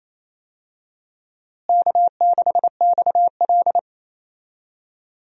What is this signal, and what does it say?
Morse code, heard as a keyed tone, about 28 wpm, 700 Hz. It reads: K6XL
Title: USA Call Signs - 4 Characters - Repeated 2x - 28wpm